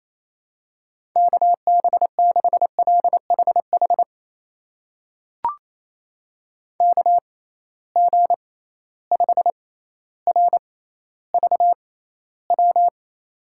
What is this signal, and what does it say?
Morse code, heard as a keyed tone, about 28 wpm, spaced as for 14 wpm, 700 Hz. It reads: KB6LHH KG5RVW
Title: USA Call Signs - 6 Characters - 6x Character Spacing - 28wpm